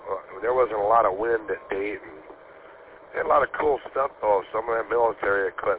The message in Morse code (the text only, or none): none